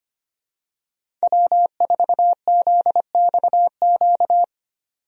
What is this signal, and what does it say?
Morse code, heard as a keyed tone, about 25 wpm, 700 Hz. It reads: W4ZXQ
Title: USA Call Signs - 5 Characters - 25wpm